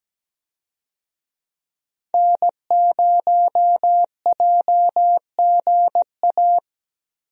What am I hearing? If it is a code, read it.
Morse code, heard as a keyed tone, about 17 wpm, 700 Hz. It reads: N0JGA
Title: USA Call Signs - 5 Characters - Repeated 3x - 17wpm